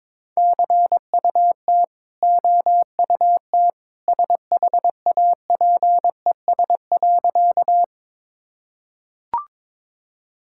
Morse code, heard as a keyed tone, about 22 wpm, 690 Hz. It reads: CUT OUT SHAPES.
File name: practice / 22wpm / Easy Sentences with Hints - 22wpm